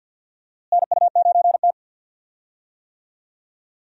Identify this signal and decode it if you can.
Morse code, heard as a keyed tone, about 47 wpm, 700 Hz. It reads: NU9T
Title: USA Call Signs - 4 Characters - Repeated 2x - 50wpm